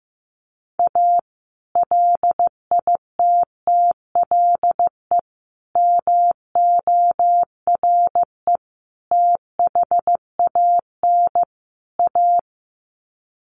Morse code, heard as a keyed tone, about 15 wpm, 700 Hz. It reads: A LITTLE MORE THAN A
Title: Sets of 5 Words Encore2 15wpm